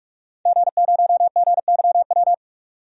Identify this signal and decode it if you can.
Morse code, heard as a keyed tone, about 41 wpm, 700 Hz. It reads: G0GYW
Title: DXCC-100 Call Signs - 5 Characters - 45wpm